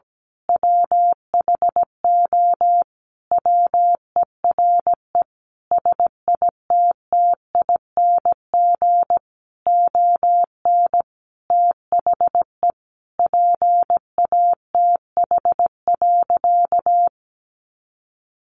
Morse code, heard as a keyed tone, about 17 wpm, 700 Hz. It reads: WHO WERE SITTING ON THE PATH.
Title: A Sound Of Thunder 17wpm